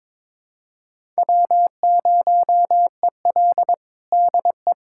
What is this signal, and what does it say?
Morse code, heard as a keyed tone, about 22 wpm, 700 Hz. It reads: W0EL DE
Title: USA Callsign DE Callsign - 4 Characters - 22wpm